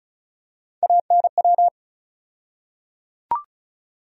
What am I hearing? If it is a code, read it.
Morse code, heard as a keyed tone, about 34 wpm, 700 Hz. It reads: ANW
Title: USA Call Sign Postfixes - 3 Letters 35wpm